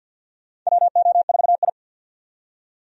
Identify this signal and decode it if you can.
Morse code, heard as a keyed tone, about 48 wpm, 700 Hz. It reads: WO4I
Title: USA Call Signs - 4 Characters - 50wpm